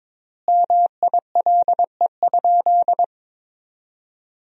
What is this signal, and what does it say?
Morse code, heard as a keyed tone, about 22 wpm, 700 Hz. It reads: MILE?
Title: 4x Word-Spacing - Sentences from Top 400 Words - 22wpm